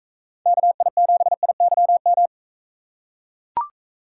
Morse code, heard as a keyed tone, about 42 wpm, 700 Hz. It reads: KI7IYM